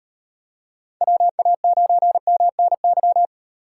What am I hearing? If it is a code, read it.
Morse code, heard as a keyed tone, about 37 wpm, 700 Hz. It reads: WA9MNY